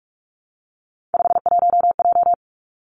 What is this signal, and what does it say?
Morse code, heard as a keyed tone, about 46 wpm, 700 Hz. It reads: 51J
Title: Three Challenging Letter-Number Combinations 45wpm